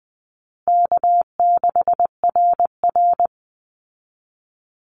Morse code, heard as a keyed tone, about 20 wpm, 700 Hz. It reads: K6RR
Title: Call Signs Repeated 2x - 20wpm